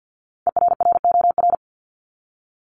Morse code, heard as a keyed tone, about 50 wpm, 700 Hz. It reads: ERROR
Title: Top 1751-2000 Words 50wpm